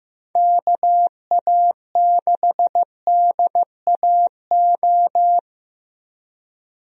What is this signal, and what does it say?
Morse code, heard as a keyed tone, about 15 wpm, 700 Hz. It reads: KA6DAO
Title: USA Call Signs - 6 Characters - Repeated 2x - 15wpm